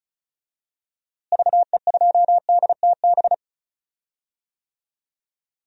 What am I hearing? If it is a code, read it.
Morse code, heard as a keyed tone, about 34 wpm, 700 Hz. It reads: VE2DTB
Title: DXCC-100 Call Signs - 6 Characters - Repeated 3x - 35wpm